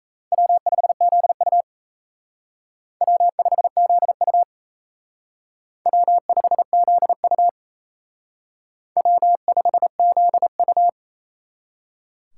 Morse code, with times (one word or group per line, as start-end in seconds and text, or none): 0.3-1.6 W5ZU
3.0-4.4 W5ZU
5.9-7.5 W5ZU
9.0-10.9 W5ZU